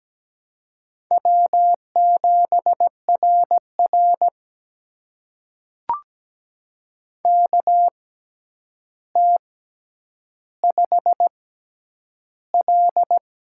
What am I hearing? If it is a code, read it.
Morse code, heard as a keyed tone, about 17 wpm, spaced as for 9 wpm, 700 Hz. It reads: W7RR KT5L
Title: USA Call Signs - 4 Characters - 6x Character Spacing - 17wpm